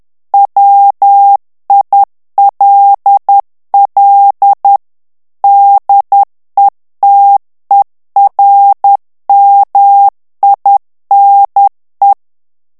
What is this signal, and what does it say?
Morse code, heard as a keyed tone, about 10 wpm, 800 Hz. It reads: WILL DETERMINE
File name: ANGLAIS texte clair 3 vitesse 10 mots minute